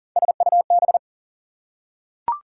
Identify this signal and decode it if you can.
Morse code, heard as a keyed tone, about 40 wpm, 700 Hz. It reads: SUB
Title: Prefixes and Suffixes 40wpm